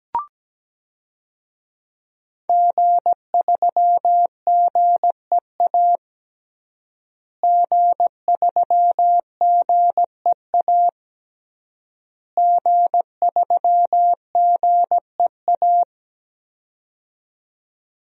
Morse code, heard as a keyed tone, about 17 wpm, 700 Hz. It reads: G3GEA G3GEA G3GEA
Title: DXCC-100 Call Signs - 5 Characters - Repeated 3x - 17wpm